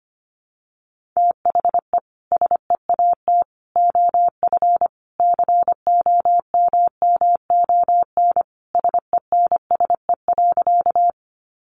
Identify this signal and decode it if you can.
Morse code, heard as a keyed tone, about 25 wpm, 700 Hz. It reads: THE SEAT OF COMMON SENSE.
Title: Sentences from Top 1500 Words 25wpm